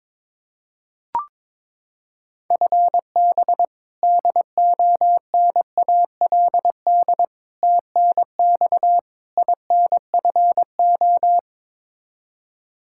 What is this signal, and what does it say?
Morse code, heard as a keyed tone, about 22 wpm, 700 Hz. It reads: FB DONALD TNX INFO
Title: Practice QSO - Exchange 3 - More Info - 22wpm